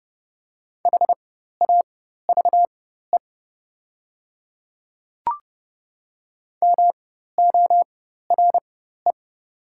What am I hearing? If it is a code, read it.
Morse code, heard as a keyed tone, about 30 wpm, spaced as for 14 wpm, 700 Hz. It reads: HAVE MORE
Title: 4x Character Spacing - Top 100 Words - 30wpm